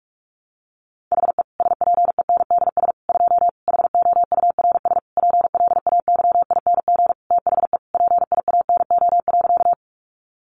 Morse code, heard as a keyed tone, about 44 wpm, 700 Hz. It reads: HE SPENDS 2 HOURS PLAYING THE PIANO.
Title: Additional Easy Phrases 45wpm